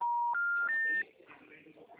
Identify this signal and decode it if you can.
Special information tone, with speech.
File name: out- -2009- - - .182